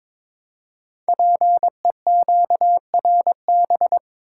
Morse code, heard as a keyed tone, about 22 wpm, 700 Hz. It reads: PEQRB
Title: Code Groups - 5 Letters - 22wpm